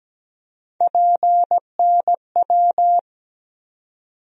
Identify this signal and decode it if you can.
Morse code, heard as a keyed tone, about 17 wpm, 700 Hz. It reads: PNW
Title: USA Call Sign Postfixes - 3 Letters 17wpm